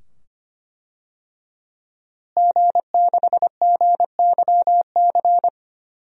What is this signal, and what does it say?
Morse code, heard as a keyed tone, about 25 wpm, 700 Hz. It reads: G6GYC
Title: DXCC-100 Call Signs - 5 Characters - Repeated 2x - 25wpm